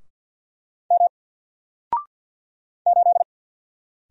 Morse code, heard as a keyed tone, about 48 wpm, 700 Hz. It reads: M 8
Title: Single Letter-Number Alt 50wpm